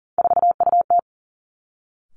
Morse code, heard as a keyed tone, about 40 wpm, 700 Hz. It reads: 4UT
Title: Three Challenging Letter-Number Combinations 40wpm